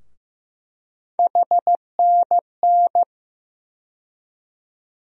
Morse code, heard as a keyed tone, about 15 wpm, 700 Hz. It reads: HNN
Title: USA Call Sign Postfixes - 3 Letters 15wpm